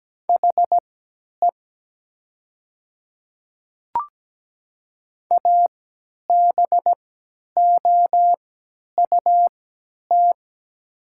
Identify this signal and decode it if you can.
Morse code, heard as a keyed tone, about 17 wpm, spaced as for 10 wpm, 700 Hz. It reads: HE ABOUT
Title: 3x Character Spacing - Top 100 Words - 17wpm